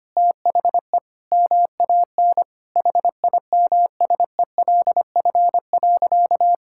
Morse code, heard as a keyed tone, about 25 wpm, 700 Hz. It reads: THE MAN HIMSELF.
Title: Sentences from Top 1600 Words 25wpm